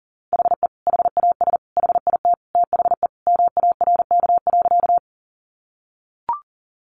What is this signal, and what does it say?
Morse code, heard as a keyed tone, about 40 wpm, 700 Hz. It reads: HE HAS HIT THE MARK.
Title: Sentences from Top 1200 Words 40wpm